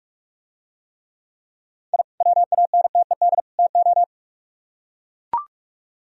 Morse code, heard as a keyed tone, about 44 wpm, 700 Hz. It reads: I WANTED TO